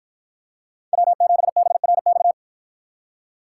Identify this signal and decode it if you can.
Morse code, heard as a keyed tone, about 53 wpm, 700 Hz. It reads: W7BRX